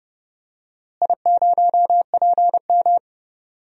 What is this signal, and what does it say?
Morse code, heard as a keyed tone, about 31 wpm, 700 Hz. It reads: I0PM
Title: DXCC-100 Call Signs - 4 Characters - Repeated 3x - 30wpm